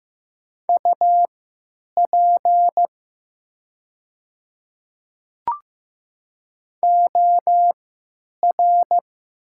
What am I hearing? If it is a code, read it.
Morse code, heard as a keyed tone, about 15 wpm, spaced as for 9 wpm, 700 Hz. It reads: UP OR
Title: 3x Character Spacing - Top 300 Words - 15wpm